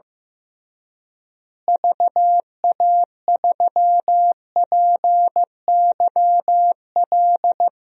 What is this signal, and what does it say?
Morse code, heard as a keyed tone, about 15 wpm, 700 Hz. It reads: VA3PYL
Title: DXCC-100 Call Signs - 6 Characters - Repeated 3x - 15wpm